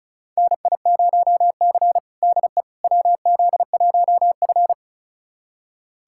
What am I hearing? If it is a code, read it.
Morse code, heard as a keyed tone, about 34 wpm, 700 Hz. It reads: NI0C DE WZ1F
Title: USA Callsign DE Callsign - 4 Characters - 35wpm